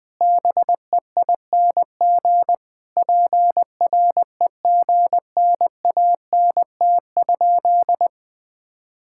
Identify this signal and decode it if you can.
Morse code, heard as a keyed tone, about 20 wpm, 700 Hz. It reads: BEING PREGNANT?
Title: Sentences from Top 1000 Words - Review Part 6 - 20wpm